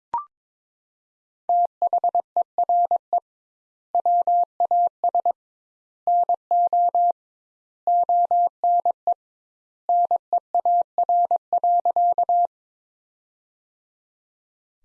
Morse code, heard as a keyed tone, about 22 wpm, 700 Hz. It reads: THERE WAS NO ONE NEAR.
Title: Sentences Top 200 Words - 22wpm